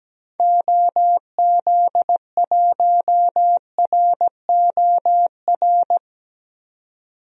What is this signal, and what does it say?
Morse code, heard as a keyed tone, about 17 wpm, 700 Hz. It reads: OZ1ROR
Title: DXCC-100 Call Signs - 6 Characters - 17wpm